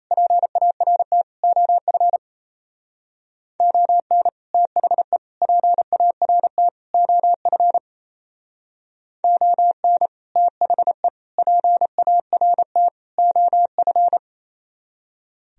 Morse code, heard as a keyed tone, about 34 wpm, 700 Hz. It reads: PART OF ON THE PART OF ON THE PART OF
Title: Speed-Racing - Sets of 4 Words 28wpm